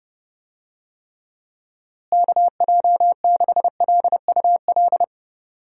Morse code, heard as a keyed tone, about 30 wpm, 700 Hz. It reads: KJ6LUL
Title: USA Call Signs - 6 Characters - Repeated 2x - 30wpm